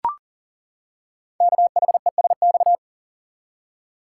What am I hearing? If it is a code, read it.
Morse code, heard as a keyed tone, about 40 wpm, 700 Hz. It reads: KHESX
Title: Code Groups - 5 Letters - 40wpm